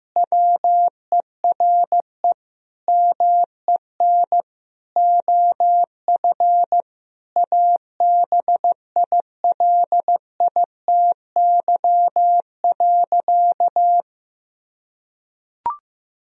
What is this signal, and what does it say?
Morse code, heard as a keyed tone, about 14 wpm, 700 Hz. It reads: WERE MEN OF ABILITY.
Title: Sentences from Top 1000 Words - Review Part 3 - 15wpm